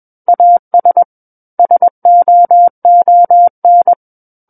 Morse code, heard as a keyed tone, about 21 wpm, 700 Hz. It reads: AS SOON
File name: Book One / 12-Chapter 12 - 21 WPM